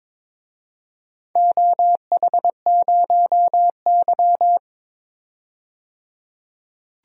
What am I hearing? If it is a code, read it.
Morse code, heard as a keyed tone, about 22 wpm, 700 Hz. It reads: OH0Y
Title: DXCC-100 Call Signs - 4 Characters - 22wpm